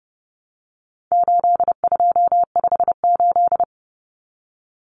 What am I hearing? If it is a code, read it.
Morse code, heard as a keyed tone, about 30 wpm, 700 Hz. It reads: 8258
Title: Common Cut Numbers - 4-Digit 30wpm